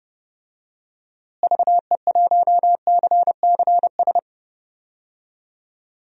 Morse code, heard as a keyed tone, about 30 wpm, 700 Hz. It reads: VE1CCS
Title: DXCC-100 Call Signs - 6 Characters - Repeated 2x - 30wpm